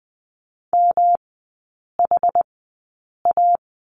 Morse code, heard as a keyed tone, about 20 wpm, spaced as for 8 wpm, 700 Hz. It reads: MHA
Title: 107 Lesson 18e - Letter m - Three Characters 20wpm